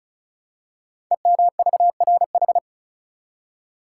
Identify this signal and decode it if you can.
Morse code, heard as a keyed tone, about 36 wpm, 690 Hz. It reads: EMVRH
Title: Code Groups - 5 Letters - 35wpm